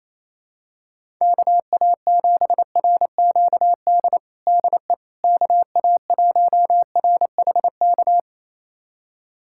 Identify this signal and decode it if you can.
Morse code, heard as a keyed tone, about 28 wpm, 700 Hz. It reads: KA7RQD DE KA1RHK
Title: USA Callsign DE Callsign - 6 Characters - 28wpm